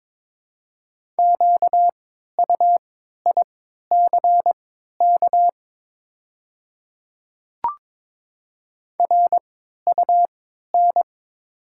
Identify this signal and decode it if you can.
Morse code, heard as a keyed tone, about 22 wpm, spaced as for 13 wpm, 700 Hz. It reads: QUICK RUN
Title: 3x Character Spacing - Top 500 Words - 22wpm